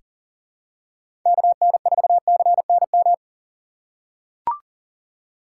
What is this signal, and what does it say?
Morse code, heard as a keyed tone, about 40 wpm, 700 Hz. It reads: KN4CNM
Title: USA Call Signs - 6 Characters - 8x Character Spacing - 40wpm